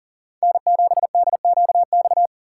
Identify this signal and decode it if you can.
Morse code, heard as a keyed tone, about 48 wpm, 700 Hz. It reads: N7DQX
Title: Speed Racing - USA 1x3 Call Signs 40wpm